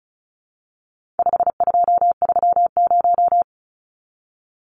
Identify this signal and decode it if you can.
Morse code, heard as a keyed tone, about 34 wpm, 700 Hz. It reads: 5230
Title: Numbers - 4-Digit 35wpm